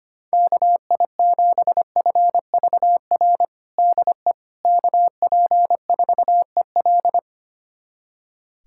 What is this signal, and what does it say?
Morse code, heard as a keyed tone, about 25 wpm, 700 Hz. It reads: KI7FVR DE KP4EL
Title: USA Callsign DE Callsign 25wpm